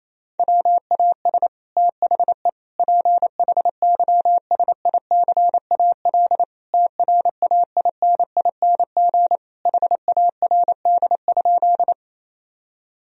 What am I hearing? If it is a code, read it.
Morse code, heard as a keyed tone, about 28 wpm, 700 Hz. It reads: WAS THE PHYSICAL TRAINING HARD?
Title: Sentences from Top 1000 Words - Review Part 4 - 28wpm